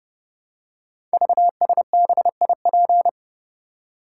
Morse code, heard as a keyed tone, about 30 wpm, 700 Hz. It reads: VSBIP
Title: Code Groups - 5 Letter:Numbers - 30wpm